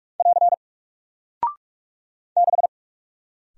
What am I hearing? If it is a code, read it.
Morse code, heard as a keyed tone, about 44 wpm, 700 Hz. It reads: <AR> 6
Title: Single Character (Letter:Number:Punctuation:Prosign) 45wpm